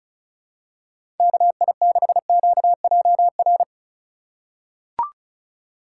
Morse code, heard as a keyed tone, about 36 wpm, 700 Hz. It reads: KI6QJR
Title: USA Call Signs - 8x Character Spacing - 35wpm